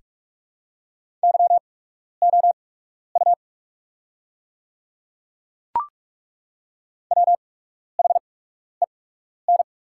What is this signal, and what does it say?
Morse code, heard as a keyed tone, about 42 wpm, spaced as for 12 wpm, 700 Hz. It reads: YOU WHEN
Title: 8x Character Spacing - Top 200 Words - 45wpm